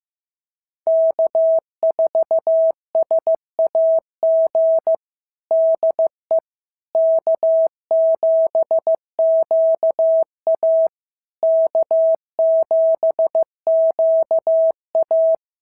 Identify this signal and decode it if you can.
Morse code, heard as a keyed tone, about 15 wpm, 650 Hz. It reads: K4SAG DE K7QA K7QA
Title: SOTA QSO Practice 15wpm